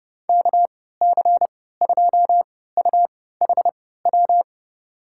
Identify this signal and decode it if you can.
Morse code, heard as a keyed tone, about 32 wpm, spaced as for 17 wpm, 700 Hz. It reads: KC2UHW
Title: USA Call Signs - 6 Characters - 3x Character Spacing - 30wpm